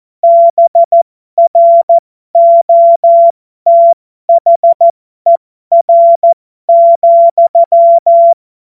Morse code, heard as a keyed tone, about 14 wpm, spaced as for 12 wpm, 680 Hz. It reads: BROTHER,